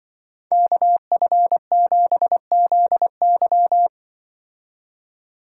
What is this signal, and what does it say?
Morse code, heard as a keyed tone, about 24 wpm, 700 Hz. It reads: KF7ZY
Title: Speed Racing - USA 2x2 Call Signs 20wpm